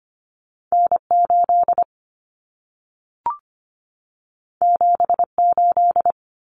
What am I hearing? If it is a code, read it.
Morse code, heard as a keyed tone, about 25 wpm, 700 Hz. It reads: N8 78
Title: Common Cut Numbers - 2-Digit 25wpm